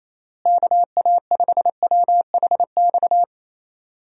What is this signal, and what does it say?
Morse code, heard as a keyed tone, about 28 wpm, 700 Hz. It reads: KA5WHX
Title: USA Call Signs - 6 Characters - Repeated 3x - 28wpm